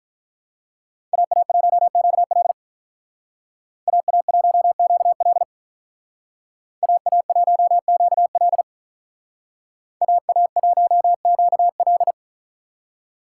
Morse code, heard as a keyed tone, about 44 wpm, 700 Hz. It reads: AA1QL AA1QL AA1QL AA1QL